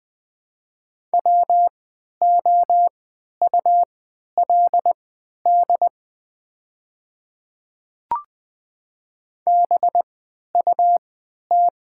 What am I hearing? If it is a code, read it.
Morse code, heard as a keyed tone, about 20 wpm, spaced as for 11 wpm, 700 Hz. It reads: WOULD BUT
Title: 3x Character Spacing - Top 200 Words - 20wpm